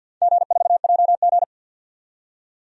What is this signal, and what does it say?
Morse code, heard as a keyed tone, about 52 wpm, 700 Hz. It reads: G4JG
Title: DXCC-100 Call Signs - 4 Characters - 50wpm